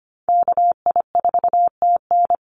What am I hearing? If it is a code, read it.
Morse code, heard as a keyed tone, about 25 wpm, 700 Hz. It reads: KI4TN
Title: NA SOTA Activator Callsigns 25wpm